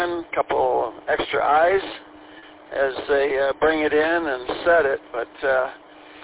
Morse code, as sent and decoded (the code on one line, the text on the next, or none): none